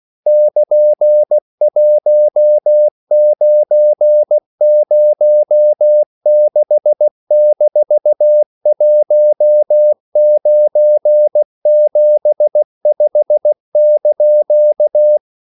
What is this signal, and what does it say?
Morse code, heard as a keyed tone, about 16 wpm, 600 Hz. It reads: <KN>1906-1975)
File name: CW / Practice / EnglishQuotations0216